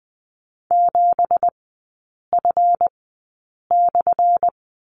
Morse code, heard as a keyed tone, about 20 wpm, spaced as for 8 wpm, 690 Hz. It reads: 7F/
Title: 198 Lesson 32e - Letter k - Three Characters 20wpm